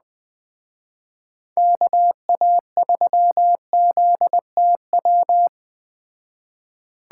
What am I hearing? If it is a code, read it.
Morse code, heard as a keyed tone, about 20 wpm, 700 Hz. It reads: KA3ZTW
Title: Speed Racing - USA 2x3 Call Signs 17wpm